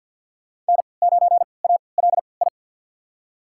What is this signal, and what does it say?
Morse code, heard as a keyed tone, about 47 wpm, spaced as for 28 wpm, 700 Hz. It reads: N9ALI
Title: USA Call Signs - 5 Characters - 3x Character Spacing - 50wpm